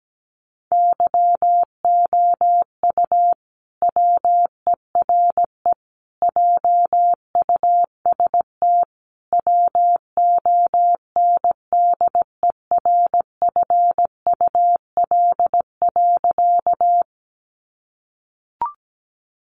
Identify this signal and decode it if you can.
Morse code, heard as a keyed tone, about 17 wpm, 700 Hz. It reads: YOU WERE JUST WONDERFUL.
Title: Sentences from Top 1100 Words 17wpm